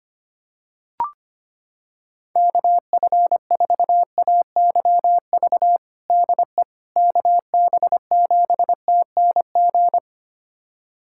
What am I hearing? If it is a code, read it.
Morse code, heard as a keyed tone, about 25 wpm, 700 Hz. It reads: KF4AYV DE KB7TNG
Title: USA Callsign DE Callsign - 6 Characters - 25wpm